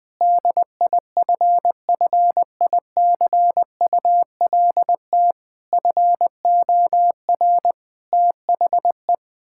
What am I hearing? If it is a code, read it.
Morse code, heard as a keyed tone, about 20 wpm, 700 Hz. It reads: DIFFICULT FOR THE
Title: Sentences from Top 1000 Words - Review Part 5 - 20wpm